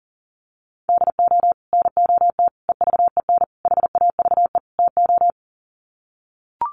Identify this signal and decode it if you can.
Morse code, heard as a keyed tone, about 40 wpm, 700 Hz. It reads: DO NOT EVEN HAVE TO
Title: Sets of 5 Words Encore2 40wpm